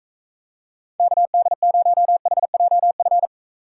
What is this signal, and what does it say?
Morse code, heard as a keyed tone, about 43 wpm, 700 Hz. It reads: KD0HJF